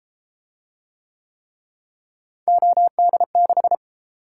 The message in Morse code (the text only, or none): OD6